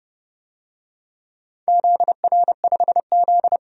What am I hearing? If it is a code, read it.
Morse code, heard as a keyed tone, about 30 wpm, 700 Hz. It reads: ZR5Z
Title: DXCC-100 Call Signs - 4 Characters - 30wpm